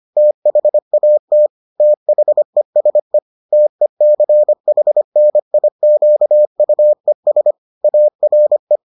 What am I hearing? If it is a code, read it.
Morse code, heard as a keyed tone, about 25 wpm, 600 Hz. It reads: THAT THESE TECHNIQUES ARE